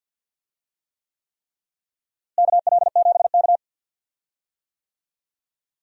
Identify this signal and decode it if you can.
Morse code, heard as a keyed tone, about 47 wpm, 700 Hz. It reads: KL7K
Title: DXCC-100 Call Signs - 4 Characters - Repeated 3x - 50wpm